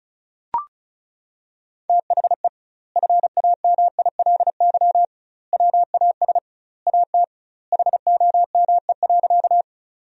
Morse code, heard as a keyed tone, about 36 wpm, 700 Hz. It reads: THE FAMILY WAS AT HOME.
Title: 2x Word-Spacing - Sentences from Top 400 Words - 35wpm